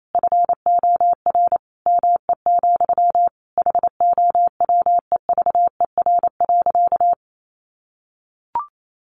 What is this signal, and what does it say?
Morse code, heard as a keyed tone, about 28 wpm, 700 Hz. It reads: FOR ME, HOWEVER.
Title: Sentences from Top 600 Words 28wpm